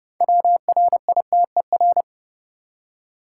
Morse code, heard as a keyed tone, about 32 wpm, 700 Hz. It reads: WRITER
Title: Six Letter Words - 30wpm